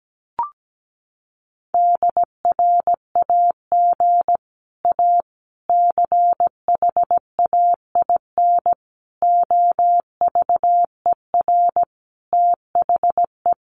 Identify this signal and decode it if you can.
Morse code, heard as a keyed tone, about 17 wpm, 700 Hz. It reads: DRAG A CHAIN OVER THE
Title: Sentences from Top 1500 Words 17wpm